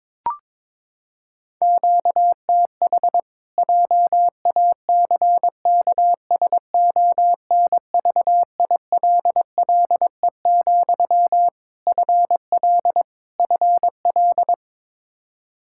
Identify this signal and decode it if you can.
Morse code, heard as a keyed tone, about 22 wpm, 700 Hz. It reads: QTH JACKSONVILLE, FL FL
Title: Practice QSO - Exchange 1 - Beginning - 22wpm